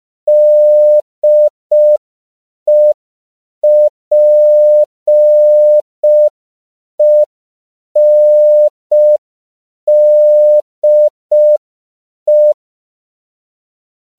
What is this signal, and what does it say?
Morse code, heard as a keyed tone, about 5 wpm, 600 Hz. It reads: DEPENDE